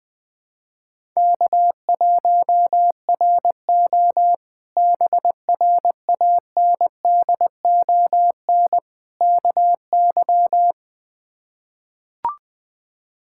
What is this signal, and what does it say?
Morse code, heard as a keyed tone, about 20 wpm, 700 Hz. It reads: K1RO BRANDON KY
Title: K1USN SST QSO Practice - 20wpm